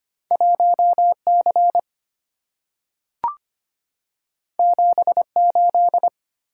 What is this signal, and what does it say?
Morse code, heard as a keyed tone, about 25 wpm, 690 Hz. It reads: 1C 78
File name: Two Letter-Number Combinations - 25wpm